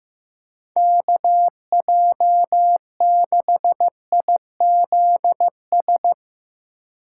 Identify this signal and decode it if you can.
Morse code, heard as a keyed tone, about 15 wpm, 700 Hz. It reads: KJ6IZS
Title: USA Call Signs - 6 Characters - 2x Character Spacing - 15wpm